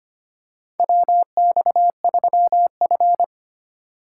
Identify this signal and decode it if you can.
Morse code, heard as a keyed tone, about 25 wpm, 700 Hz. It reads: WX3F
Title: USA Call Signs - 4x Character Spacing - 25wpm